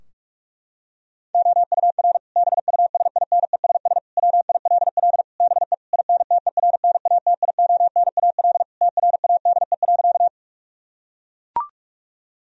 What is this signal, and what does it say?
Morse code, heard as a keyed tone, about 41 wpm, 700 Hz. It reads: OUR BUSINESS WILL BE INTERNATIONAL TRADE.
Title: Sentences from Top 1000 Words - Review Part 4 - 45wpm